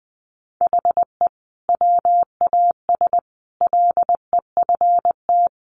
Morse code, heard as a keyed tone, about 20 wpm, 700 Hz. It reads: HE WAS LEFT
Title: Sentences from Top 700 Words 20wpm